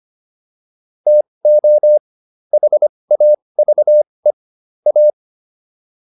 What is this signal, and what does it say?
Morse code, heard as a keyed tone, about 25 wpm, spaced as for 20 wpm, 600 Hz. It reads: TO HAVE A